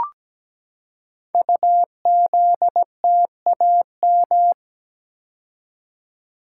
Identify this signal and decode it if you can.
Morse code, heard as a keyed tone, about 17 wpm, 700 Hz. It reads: UZTAM
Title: Code Groups - 5 Letters - 17wpm